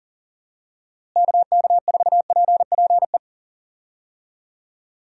Morse code, heard as a keyed tone, about 40 wpm, 700 Hz. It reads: KK4PPE
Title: USA Call Signs - 6 Characters - Repeated 3x - 40wpm